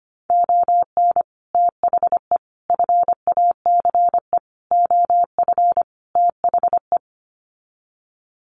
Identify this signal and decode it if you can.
Morse code, heard as a keyed tone, about 25 wpm, 700 Hz. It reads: ON THE FACE OF THE
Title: Sets of 5 Words Encore2 25wpm